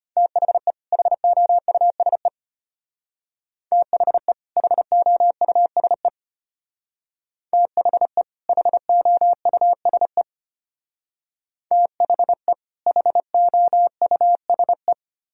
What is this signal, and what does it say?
Morse code, heard as a keyed tone, about 31 wpm, 700 Hz. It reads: THE HOUSE THE HOUSE THE HOUSE THE HOUSE
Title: Speed-Racing - Sets of 2 Words 25wpm